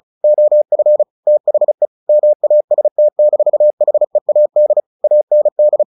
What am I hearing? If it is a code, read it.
Morse code, heard as a keyed tone, about 35 wpm, 600 Hz. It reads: OF THE MAST-HEAD AND
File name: the old man and the sea 2 35wpm